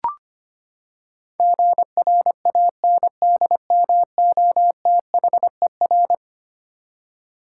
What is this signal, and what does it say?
Morse code, heard as a keyed tone, about 25 wpm, 700 Hz. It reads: GRANDMOTHER
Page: morse-code-ninja-practice-25wpm